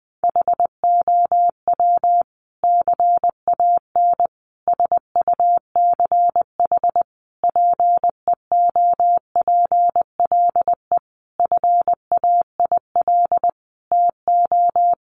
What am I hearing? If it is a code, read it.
Morse code, heard as a keyed tone, about 20 wpm, 700 Hz. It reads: HOW CAN SUCH PEOPLE FAIL TO
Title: Sentences from Top 1100 Words 20wpm